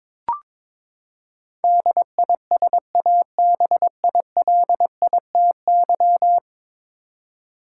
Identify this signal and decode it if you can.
Morse code, heard as a keyed tone, about 22 wpm, 700 Hz. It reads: DISABILITY
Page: morse-code-ninja-practice-22wpm